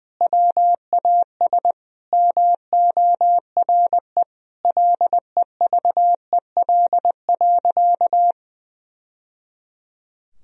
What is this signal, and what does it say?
Morse code, heard as a keyed tone, about 20 wpm, 700 Hz. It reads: WAS MORE LEVEL.
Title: Sentences from Top 1000 Words - Review Part 4 - 20wpm